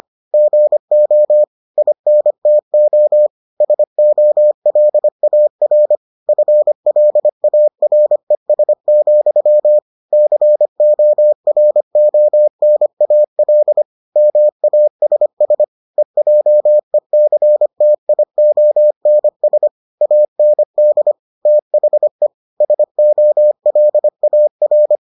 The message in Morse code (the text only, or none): GO INTO SOLAR FLARES, CORONAL MASS EJECTIONS AND THE SOLAR